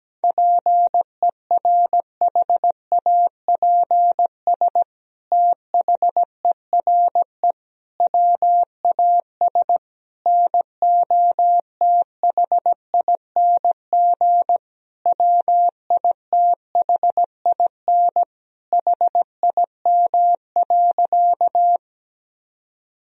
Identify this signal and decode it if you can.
Morse code, heard as a keyed tone, about 17 wpm, 700 Hz. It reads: PERHAPS THERE WAS NOTHING WITHIN HIM.
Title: Sentences from Top 1000 Words - Review Part 4 - 17wpm